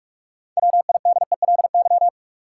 Morse code, heard as a keyed tone, about 44 wpm, 700 Hz. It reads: WIDELY
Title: Sentences from Top 1200 Words - New Word Drill Alt - 45wpm